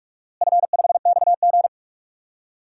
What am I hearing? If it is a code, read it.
Morse code, heard as a keyed tone, about 46 wpm, 700 Hz. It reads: F5XG
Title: DXCC-100 Call Signs - 4 Characters - Repeated 2x - 45wpm